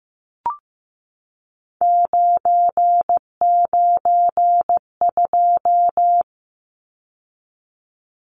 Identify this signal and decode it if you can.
Morse code, heard as a keyed tone, about 15 wpm, 700 Hz. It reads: 992
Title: Numbers - 3-Digit 15wpm